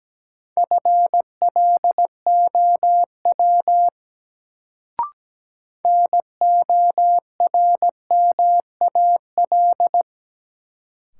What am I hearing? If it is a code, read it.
Morse code, heard as a keyed tone, about 17 wpm, 700 Hz. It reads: FLOW NORMAL
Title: Sentences from Top 1200 Words - New Word Drill Alt - 17wpm